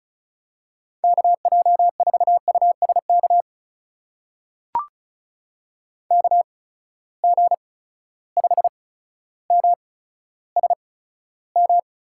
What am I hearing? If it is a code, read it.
Morse code, heard as a keyed tone, about 36 wpm, spaced as for 15 wpm, 700 Hz. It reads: KJ4USK KG5MSM
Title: USA Call Signs - 6 Characters - 8x Character Spacing - 35wpm